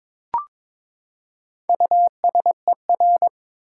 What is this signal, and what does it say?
Morse code, heard as a keyed tone, about 22 wpm, 700 Hz. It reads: USER
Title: Four Letter Words - 22wpm